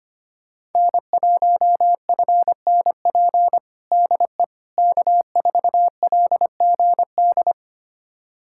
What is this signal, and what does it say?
Morse code, heard as a keyed tone, about 25 wpm, 700 Hz. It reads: N1FNP DE K4LGD
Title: USA Callsign DE Callsign - 5 Characters - 25wpm